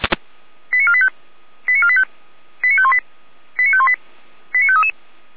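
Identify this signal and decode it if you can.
Five-tone selective call (ZVEI), heard as quick runs of five tones, tones of about 70 ms. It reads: 89485 89485 89518 89518 89533
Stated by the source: ZVEI five-tone selective call received on SDR, from the Feuerwehr Orlamuende audio archive.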